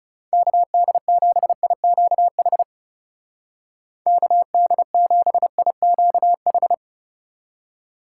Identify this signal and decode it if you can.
Morse code, heard as a keyed tone, about 40 wpm, 700 Hz. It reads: KD7IQH KD7IQH